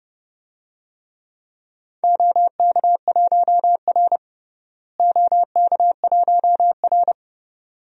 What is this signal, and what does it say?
Morse code, heard as a keyed tone, about 30 wpm, 700 Hz. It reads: OK1R OK1R
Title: DXCC-100 Call Signs - 4 Characters - Repeated 2x - 30wpm